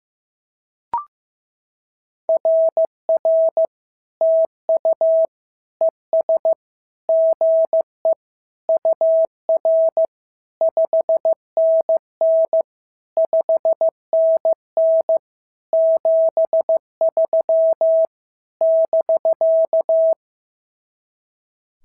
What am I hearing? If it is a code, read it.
Morse code, heard as a keyed tone, about 14 wpm, 650 Hz. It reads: RR TU ES GE UR 5NN 5NN 73 <BK>
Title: SOTA QSO Practice 15wpm